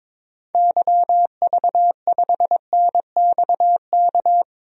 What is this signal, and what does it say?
Morse code, heard as a keyed tone, about 22 wpm, 700 Hz. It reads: YV5NXK
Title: DXCC-100 Call Signs - 6 Characters - Repeated 2x - 22wpm